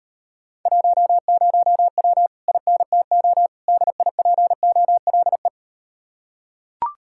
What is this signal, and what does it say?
Morse code, heard as a keyed tone, about 38 wpm, 700 Hz. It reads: 10W INTO DIPOLE